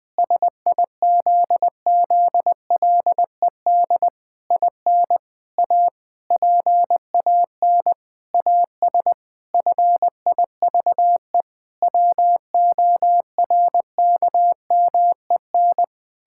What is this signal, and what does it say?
Morse code, heard as a keyed tone, about 19 wpm, 700 Hz. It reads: SIZZLED IN A PAN AS FIVE WORKMEN
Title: Pangram Challenge v2 - 20wpm